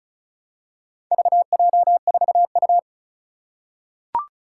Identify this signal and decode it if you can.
Morse code, heard as a keyed tone, about 36 wpm, 700 Hz. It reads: VJ4U
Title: Four Challenging Letter-Number Combinations 35wpm